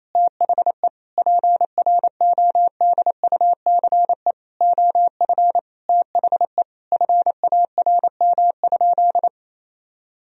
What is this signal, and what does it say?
Morse code, heard as a keyed tone, about 27 wpm, 700 Hz. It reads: THE PRODUCE OF THE FARM?
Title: Sentences from Top 1000 Words - Review Part 3 - 28wpm